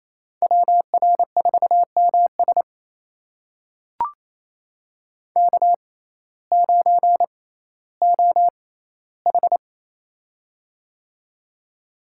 Morse code, heard as a keyed tone, about 28 wpm, spaced as for 15 wpm, 700 Hz. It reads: WR4MS K9OH
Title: USA Call Signs - 6x Character Spacing - 28wpm